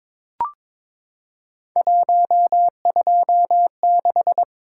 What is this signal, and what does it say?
Morse code, heard as a keyed tone, about 22 wpm, 700 Hz. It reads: 126
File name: practice / 22wpm / Three Letter-Number Combinations - 22wpm